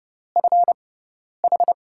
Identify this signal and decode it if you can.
Morse code, heard as a keyed tone, about 30 wpm, spaced as for 10 wpm, 700 Hz. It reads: FH
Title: USA Call Signs - 6x Character Spacing - 30wpm